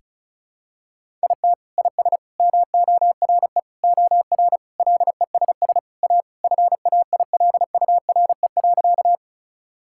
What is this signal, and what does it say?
Morse code, heard as a keyed tone, about 36 wpm, 700 Hz. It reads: IT IS MORE OR LESS A FAILURE.
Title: Sentences from Top 1000 Words - Review Part 6 - 35wpm